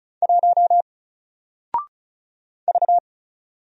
Morse code, heard as a keyed tone, about 36 wpm, 700 Hz. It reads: 1 V
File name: Single-Letter-Number-35wpm